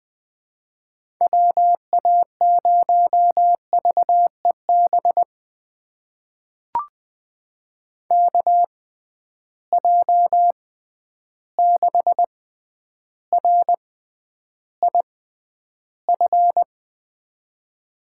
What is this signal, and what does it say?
Morse code, heard as a keyed tone, about 20 wpm, spaced as for 10 wpm, 700 Hz. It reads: WA0VEB KJ6RIF
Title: USA Call Signs - 6 Characters - 6x Character Spacing - 20wpm